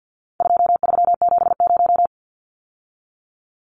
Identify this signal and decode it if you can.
Morse code, heard as a keyed tone, about 56 wpm, 700 Hz. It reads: 2370
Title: Numbers - 4-Digit 50wpm